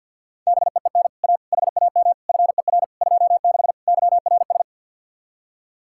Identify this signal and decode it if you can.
Morse code, heard as a keyed tone, about 64 wpm, 700 Hz. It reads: BEEN A HAM FER 26 YRS